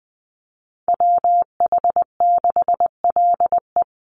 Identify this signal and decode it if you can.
Morse code, heard as a keyed tone, about 20 wpm, 700 Hz. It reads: WH6LE
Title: NA SOTA Chaser Callsigns 20wpm